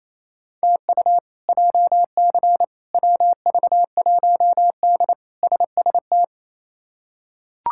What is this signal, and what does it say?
Morse code, heard as a keyed tone, about 28 wpm, 700 Hz. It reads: TU JC WV1D SST
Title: K1USN SST QSO Practice - 28wpm